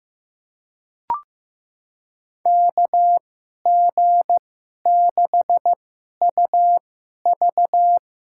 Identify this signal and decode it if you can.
Morse code, heard as a keyed tone, about 15 wpm, spaced as for 11 wpm, 700 Hz. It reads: KG6UV
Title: USA Call Signs - 5 Characters - 2x Character Spacing - 15wpm